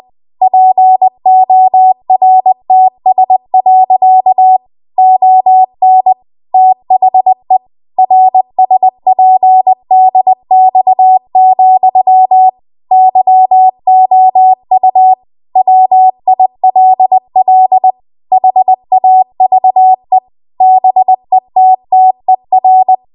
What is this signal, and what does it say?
Morse code, heard as a keyed tone, about 20 wpm, 750 Hz. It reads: PORTS. ON THE RSPDX, YOU WILL HAVE BETTER
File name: Archive / 20 WPM / 20WPM